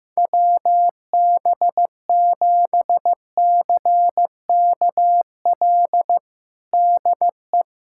Morse code, heard as a keyed tone, about 15 wpm, 700 Hz. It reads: WB7CKL DE
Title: USA Callsign DE Callsign - 6 Characters - 15wpm